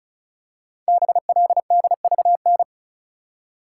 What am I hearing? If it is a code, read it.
Morse code, heard as a keyed tone, about 36 wpm, 700 Hz. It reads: BLDVN
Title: Code Groups - 5 Letter:Numbers - 35wpm